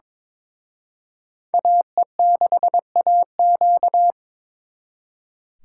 Morse code, heard as a keyed tone, about 26 wpm, 700 Hz. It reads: AE6AQ
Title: Speed Racing - USA 2x2 Call Signs 22wpm